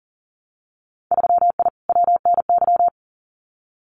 Morse code, heard as a keyed tone, about 40 wpm, 700 Hz. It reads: 3I WNY
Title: Winter Field Day Exchanges 40wpm